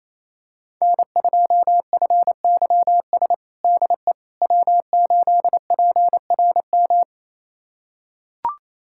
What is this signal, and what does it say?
Morse code, heard as a keyed tone, about 28 wpm, 700 Hz. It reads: N2FYS DE W8PRM
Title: USA Callsign DE Callsign - 5 Characters - 28wpm